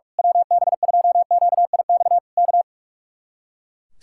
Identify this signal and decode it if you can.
Morse code, heard as a keyed tone, about 41 wpm, 700 Hz. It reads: WB2QIX K
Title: Speed-Racing Common QSO Phrases Encore - 45wpm